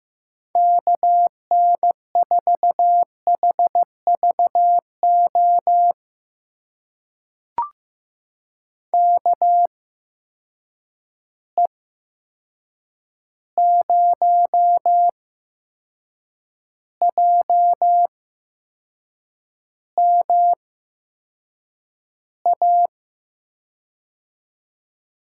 Morse code, heard as a keyed tone, about 15 wpm, spaced as for 6 wpm, 700 Hz. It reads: KN4HVO KE0JMA
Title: USA Call Signs - 6 Characters - 8x Character Spacing - 15wpm